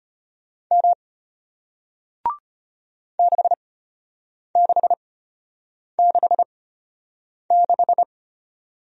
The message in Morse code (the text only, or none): M 6666